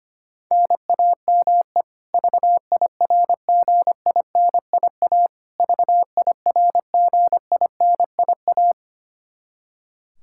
Morse code, heard as a keyed tone, about 25 wpm, 700 Hz. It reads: NAME VIRGINIA VIRGINIA
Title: Practice QSO - Exchange 1 - Beginning - 25wpm